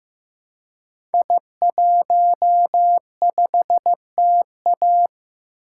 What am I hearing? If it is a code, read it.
Morse code, heard as a keyed tone, about 15 wpm, 700 Hz. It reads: I15TA
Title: Code Groups - 5 Letter:Numbers - 15wpm